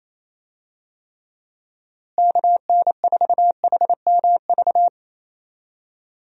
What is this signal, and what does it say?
Morse code, heard as a keyed tone, about 27 wpm, 700 Hz. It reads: KN4HMV
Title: USA Call Signs - 6 Characters - Repeated 3x - 28wpm